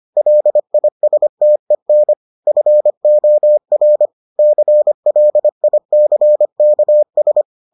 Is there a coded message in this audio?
Yes: Morse code from 0.2 to 7.4 s, heard as a keyed tone, about 25 wpm, 600 Hz. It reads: LISTEN FOR CLICKS